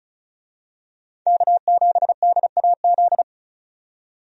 Morse code, heard as a keyed tone, about 34 wpm, 700 Hz. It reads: K7DAZ
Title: USA Call Signs - 5 Characters - 35wpm